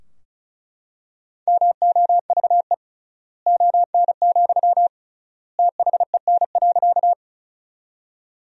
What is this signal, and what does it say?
Morse code, heard as a keyed tone, about 36 wpm, 700 Hz. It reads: MOVE ON, THEN.
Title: 3x Word-Spacing - Sentences from Top 200 Words - 35wpm